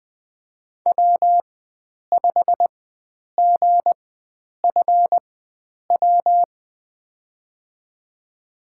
Morse code, heard as a keyed tone, about 20 wpm, spaced as for 9 wpm, 700 Hz. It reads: W5GFW
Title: USA Call Signs - 5 Characters - 4x Character Spacing - 20wpm